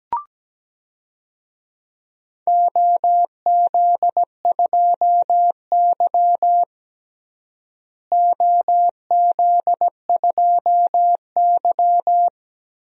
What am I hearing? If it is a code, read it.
Morse code, heard as a keyed tone, about 17 wpm, 700 Hz. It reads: OZ2Y OZ2Y
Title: DXCC-100 Call Signs - 4 Characters - Repeated 2x - 17wpm